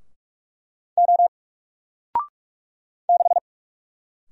Morse code, heard as a keyed tone, about 44 wpm, 700 Hz. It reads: O 6
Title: Single Letter-Number 45wpm